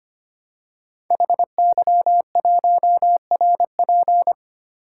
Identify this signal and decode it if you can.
Morse code, heard as a keyed tone, about 25 wpm, 700 Hz. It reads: HY1RP
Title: Code Groups - 5 Letter:Numbers - 25wpm